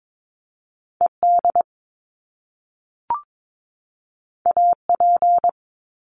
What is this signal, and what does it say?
Morse code, heard as a keyed tone, about 22 wpm, 700 Hz. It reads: ED AP
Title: Top Two Letter Combinations in Words 22wpm